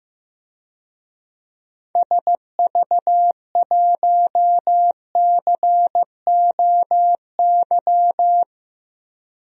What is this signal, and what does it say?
Morse code, heard as a keyed tone, about 15 wpm, 700 Hz. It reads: SV1COY
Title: DXCC-100 Call Signs - 6 Characters - 15wpm